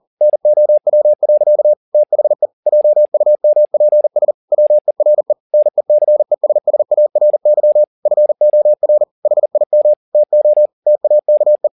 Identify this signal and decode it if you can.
Morse code, heard as a keyed tone, about 40 wpm, 600 Hz. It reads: NOW. THE JUMPS WERE NECESSARY FOR HIM TO TAKE
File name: the old man and the sea 2 40wpm